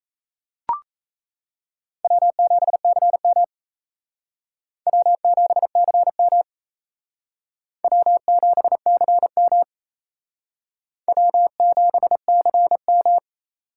Morse code, heard as a keyed tone, about 33 wpm, 700 Hz. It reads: W7CM W7CM W7CM W7CM
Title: Speed Racing - USA Call Signs 28wpm